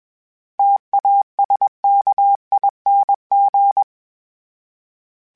Morse code, heard as a keyed tone, about 21 wpm, 800 Hz. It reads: TASKING
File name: Special INGs / Special INGs-mp3s / Sngl Ing @20